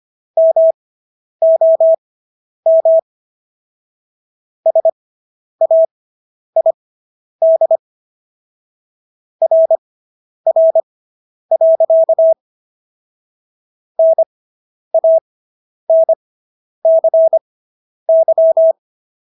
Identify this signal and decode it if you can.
Morse code, heard as a keyed tone, about 25 wpm, spaced as for 10 wpm, 650 Hz. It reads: MOM SAID RR. NANCY